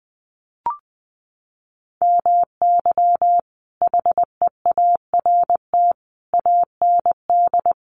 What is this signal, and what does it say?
Morse code, heard as a keyed tone, about 20 wpm, 700 Hz. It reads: MY HEART AND
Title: Sentences from Top 1200 Words 20wpm